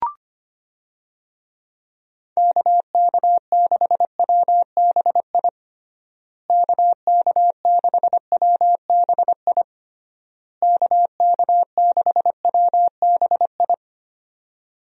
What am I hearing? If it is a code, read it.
Morse code, heard as a keyed tone, about 25 wpm, 700 Hz. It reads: KK6WBI KK6WBI KK6WBI
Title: USA Call Signs - 6 Characters - Repeated 3x - 25wpm